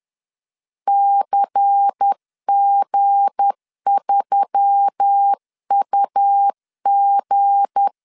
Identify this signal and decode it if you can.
Morse code, heard as a keyed tone, about 10 wpm, 800 Hz. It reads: CG3UG
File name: LEZIONI / 7 GF9Z / 7 G 10wpm 2501